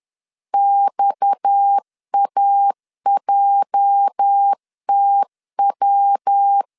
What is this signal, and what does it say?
Morse code, heard as a keyed tone, about 10 wpm, 800 Hz. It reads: XAJTW
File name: LEZIONI / 8 YJ45 / 8 YJ 10wpm 2651